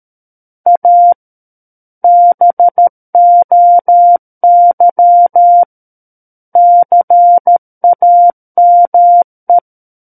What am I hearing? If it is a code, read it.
Morse code, heard as a keyed tone, about 13 wpm, 700 Hz. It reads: A BOY CAME
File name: Book One / 04-Chapter 4 - 13 WPM